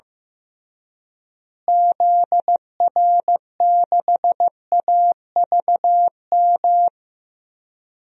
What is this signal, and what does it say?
Morse code, heard as a keyed tone, about 15 wpm, 700 Hz. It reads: ZR6AVM
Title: DXCC-100 Call Signs - 6 Characters - Repeated 3x - 15wpm